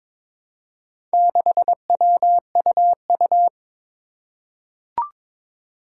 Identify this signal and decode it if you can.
Morse code, heard as a keyed tone, about 22 wpm, 700 Hz. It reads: 6WUU
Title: Four Challenging Letter-Number Combinations 22wpm